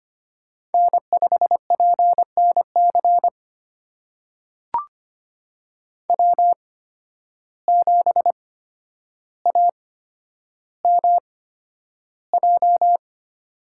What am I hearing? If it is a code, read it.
Morse code, heard as a keyed tone, about 25 wpm, spaced as for 11 wpm, 700 Hz. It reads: N5PNC W7AMJ
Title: USA Call Signs - 5 Characters - 8x Character Spacing - 25wpm